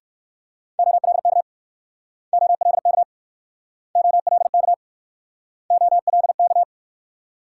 Morse code, heard as a keyed tone, about 60 wpm, 700 Hz. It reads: OLK OLK OLK OLK